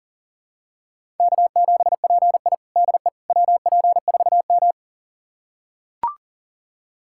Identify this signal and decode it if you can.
Morse code, heard as a keyed tone, about 40 wpm, 700 Hz. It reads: K7PI DE WP4M
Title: USA Callsign DE Callsign - 4 Characters - 40wpm